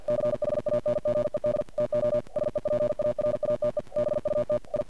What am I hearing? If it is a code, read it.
Morse code, heard as a keyed tone, about 50 wpm, 600 Hz. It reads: CHANGED TO SEPARATE BUT I
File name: archived / CRSnet